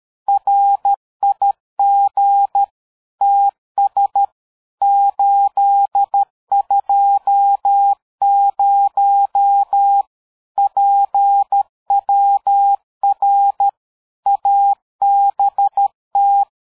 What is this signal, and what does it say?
Morse code, heard as a keyed tone, about 13 wpm, 800 Hz. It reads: RIG TS 820 PWR ABT
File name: dk5ke / QSO 60BpM